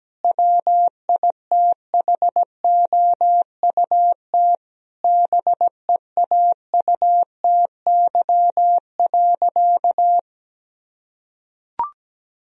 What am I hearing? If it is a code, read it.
Morse code, heard as a keyed tone, about 17 wpm, 700 Hz. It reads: WITHOUT BEAUTY.
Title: Sentences from Top 1000 Words - Review Part 3 - 17wpm